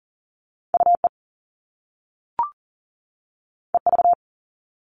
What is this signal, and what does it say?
Morse code, heard as a keyed tone, about 42 wpm, 700 Hz. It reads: UE EV
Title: All Cut Numbers - 2-Digit 40wpm